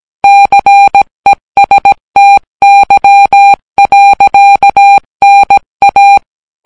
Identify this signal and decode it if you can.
Morse code, heard as a keyed tone, about 17 wpm, 800 Hz. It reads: CESTY. NA